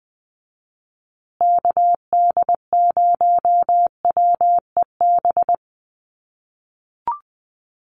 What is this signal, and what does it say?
Morse code, heard as a keyed tone, about 20 wpm, 700 Hz. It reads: KD0WEB
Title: USA Call Signs Repeated 2x 20wpm